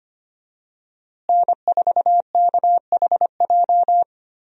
Morse code, heard as a keyed tone, about 25 wpm, 700 Hz. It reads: N4KHJ
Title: USA Call Signs - 5 Characters - Repeated 2x - 25wpm